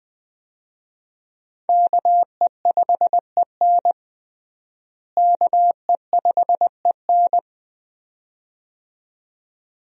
Morse code, heard as a keyed tone, about 21 wpm, 700 Hz. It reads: KE5EN KE5EN
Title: USA Call Signs - 5 Characters - Repeated 2x - 20wpm